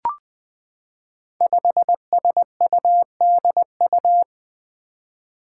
Morse code, heard as a keyed tone, about 21 wpm, 700 Hz. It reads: 5SUDU
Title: Five Challenging Letter-Number Combinations 20wpm